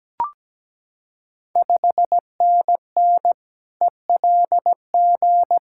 Morse code, heard as a keyed tone, about 17 wpm, 700 Hz. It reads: 5NN ELG
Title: Ontario QSO Party Exchange 17wpm